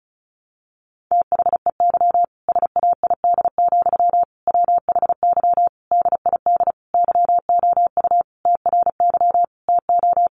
Morse code, heard as a keyed tone, about 34 wpm, 700 Hz. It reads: THEY SAID, WHY DID YOU TRY TO
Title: Sentences from Top 1200 Words 35wpm